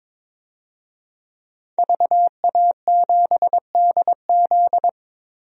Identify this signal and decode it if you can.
Morse code, heard as a keyed tone, about 22 wpm, 700 Hz. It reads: VA7DZ
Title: DXCC-100 Call Signs - 5 Characters - 22wpm